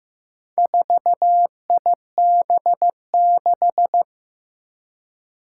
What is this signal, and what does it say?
Morse code, heard as a keyed tone, about 15 wpm, 700 Hz. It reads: 4IB6
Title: Four Challenging Letter-Number Combinations 15wpm